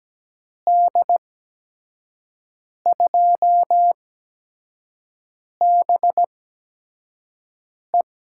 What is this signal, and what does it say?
Morse code, heard as a keyed tone, about 17 wpm, spaced as for 5 wpm, 700 Hz. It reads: D2BE
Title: USA Call Signs - 5 Characters - 8x Character Spacing - 17wpm